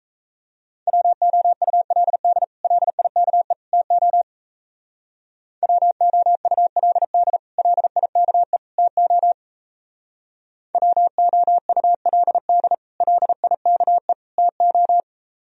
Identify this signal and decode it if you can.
Morse code, heard as a keyed tone, about 34 wpm, 700 Hz. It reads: WOULD LIKE TO WOULD LIKE TO WOULD LIKE TO
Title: Speed-Racing - Sets of 3 Words 28wpm